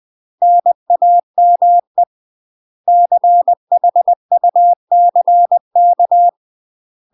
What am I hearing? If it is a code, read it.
Morse code, heard as a keyed tone, about 20 wpm, 700 Hz. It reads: NAME CHUCK
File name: UK MASTER QSOs TRAINING / 20WPM / UK MASTER QSOs 0058